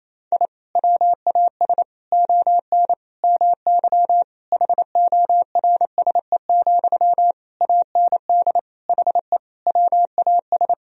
Morse code, heard as a keyed tone, about 28 wpm, 700 Hz. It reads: I WAS ON MY HORSE, AND HE WAS
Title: Sentences from Top 1000 Words - Review Part 3 - 28wpm